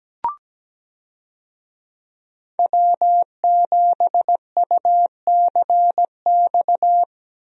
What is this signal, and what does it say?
Morse code, heard as a keyed tone, about 17 wpm, 700 Hz. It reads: W7UCX
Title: USA Call Signs - 5 Characters - Repeated 3x - 17wpm